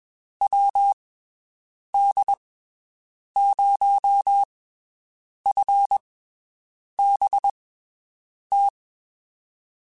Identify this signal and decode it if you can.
Morse code, heard as a keyed tone, about 21 wpm, spaced as for 7 wpm, 800 Hz. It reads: WD0FBT
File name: (spaced) / QSOs Spaced-mp3s / WD0FBT 1 @20 sp